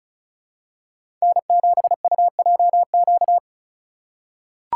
Morse code, heard as a keyed tone, about 34 wpm, 700 Hz. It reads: N7UJQ
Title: USA Call Signs - 3x Character Spacing - 35wpm